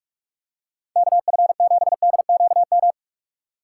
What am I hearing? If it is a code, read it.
Morse code, heard as a keyed tone, about 46 wpm, 700 Hz. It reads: KF7DQM
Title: USA Call Signs - 2x Character Spacing - 45wpm